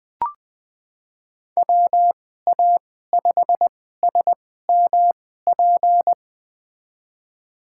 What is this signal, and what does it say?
Morse code, heard as a keyed tone, about 20 wpm, spaced as for 14 wpm, 700 Hz. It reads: WA5SMP
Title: USA Call Signs - 6 Characters - 2x Character Spacing - 20wpm